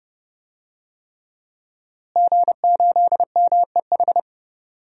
Morse code, heard as a keyed tone, about 31 wpm, 700 Hz. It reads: G8MEH
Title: DXCC-100 Call Signs - 5 Characters - Repeated 3x - 30wpm